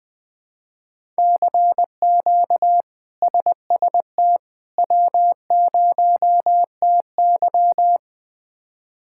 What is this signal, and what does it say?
Morse code, heard as a keyed tone, about 20 wpm, 700 Hz. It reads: CQ SST W0TY
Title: K1USN SST QSO Practice - 20wpm